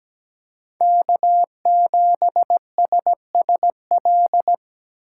Morse code, heard as a keyed tone, about 17 wpm, 700 Hz. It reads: K7SSL
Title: USA Call Signs - 2x Character Spacing - 17wpm